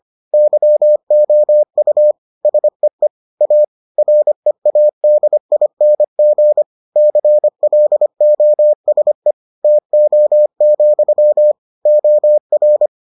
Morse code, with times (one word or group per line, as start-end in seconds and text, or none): 0.3-2.1 YOU
2.5-3.1 SEE
3.4-3.6 A
4.0-6.6 READING
7.0-9.3 CLOSE
9.6-11.5 TO,
11.9-12.9 OR